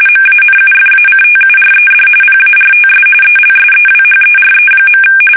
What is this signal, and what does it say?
RTTY, about 45.45 baud. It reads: MANUFACTURED PARTS ELSEWHERE.
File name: fsk850 / battleshipnj